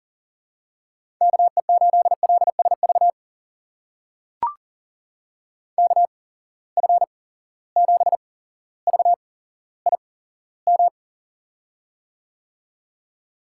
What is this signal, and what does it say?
Morse code, heard as a keyed tone, about 40 wpm, spaced as for 17 wpm, 700 Hz. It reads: KE8LSV KF7VIM
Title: USA Call Signs - 6 Characters - 8x Character Spacing - 40wpm